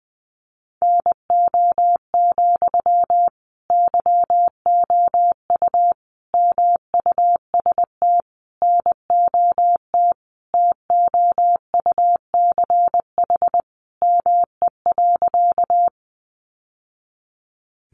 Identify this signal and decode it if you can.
Morse code, heard as a keyed tone, about 20 wpm, 700 Hz. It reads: NO, YOU MUST NOT TOUCH ME.
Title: Sentences from Top 1200 Words 20wpm